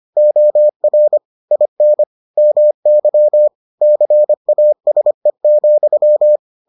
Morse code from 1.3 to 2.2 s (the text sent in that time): IN